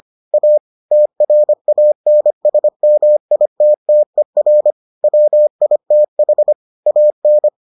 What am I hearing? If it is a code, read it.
Morse code, heard as a keyed tone, about 25 wpm, 600 Hz. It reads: A TRANSMITTER WITH AN